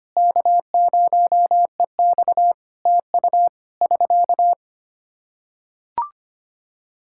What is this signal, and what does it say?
Morse code, heard as a keyed tone, about 25 wpm, 700 Hz. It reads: K0EX TU <SK>
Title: K1USN SST QSO Practice 25wpm